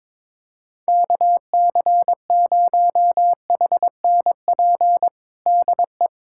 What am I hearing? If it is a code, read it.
Morse code, heard as a keyed tone, about 22 wpm, 700 Hz. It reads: KC0HNP DE
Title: USA Callsign DE Callsign 22wpm